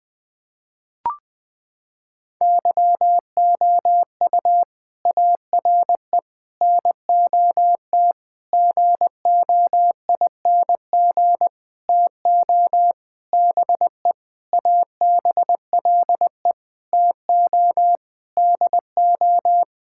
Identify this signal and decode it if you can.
Morse code, heard as a keyed tone, about 21 wpm, 700 Hz. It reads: YOU ARE NOT GOING TO BE ABLE TO DO
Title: Sentences with 7-Word Phrases - 20wpm